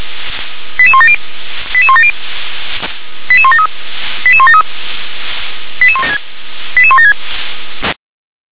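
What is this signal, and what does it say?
Five-tone selective call (ZVEI), heard as quick runs of five tones, tones of about 70 ms. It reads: 88170 88170 88173 88173 88176
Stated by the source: ZVEI five-tone selective call 88176 received with an SDR.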